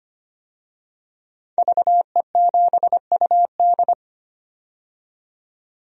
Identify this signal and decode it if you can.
Morse code, heard as a keyed tone, about 25 wpm, 700 Hz. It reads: VE7UD
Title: DXCC-100 Call Signs - 5 Characters - 25wpm